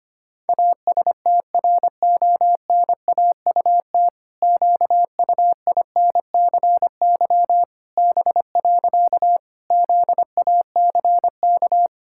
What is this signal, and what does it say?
Morse code, heard as a keyed tone, about 25 wpm, 700 Hz. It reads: ASTRONAUT QUINCY B. ZACK
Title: Pangram Challenge v2 - 25wpm